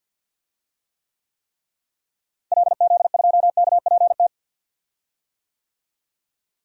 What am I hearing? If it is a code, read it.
Morse code, heard as a keyed tone, about 46 wpm, 700 Hz. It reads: LZ2KPT
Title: DXCC-100 Call Signs - 6 Characters - Repeated 3x - 50wpm